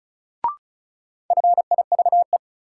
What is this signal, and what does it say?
Morse code, heard as a keyed tone, about 36 wpm, 700 Hz. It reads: FIVE
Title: Sentences from Top 400 Words - New Word Drill - 35wpm